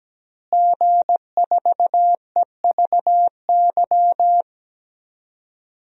Morse code, heard as a keyed tone, about 17 wpm, 700 Hz. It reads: G4EVY
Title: DXCC-100 Call Signs - 5 Characters - Repeated 3x - 17wpm